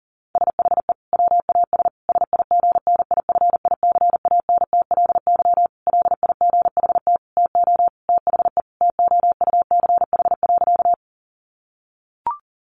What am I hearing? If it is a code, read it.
Morse code, heard as a keyed tone, about 40 wpm, 700 Hz. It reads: SHE WAS SIGNIFICANTLY LIGHT TO THE TOUCH.
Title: Sentences from Top 1500 Words 40wpm